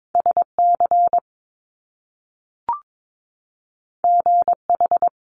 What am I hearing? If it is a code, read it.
Morse code, heard as a keyed tone, about 22 wpm, 700 Hz. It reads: SC GH
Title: Top Two Letter Combinations in Words 22wpm